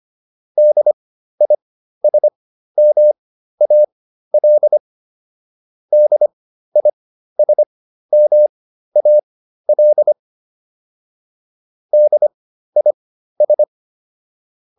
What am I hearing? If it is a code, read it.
Morse code, heard as a keyed tone, about 25 wpm, spaced as for 13 wpm, 600 Hz. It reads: DISMAL DISMAL DIS